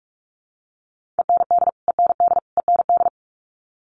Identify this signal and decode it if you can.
Morse code, heard as a keyed tone, about 44 wpm, 700 Hz. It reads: END END END
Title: Top 200 Words Repeated 45wpm